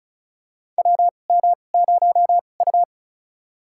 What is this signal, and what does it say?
Morse code, heard as a keyed tone, about 36 wpm, spaced as for 25 wpm, 700 Hz. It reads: WM0U
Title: USA Call Signs - 4 Characters - 2x Character Spacing - 35wpm